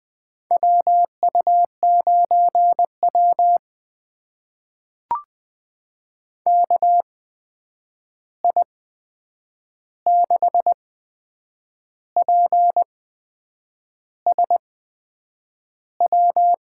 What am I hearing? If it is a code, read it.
Morse code, heard as a keyed tone, about 20 wpm, spaced as for 8 wpm, 700 Hz. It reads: WU9W KI6PSW
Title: USA Call Signs - 8x Character Spacing - 20wpm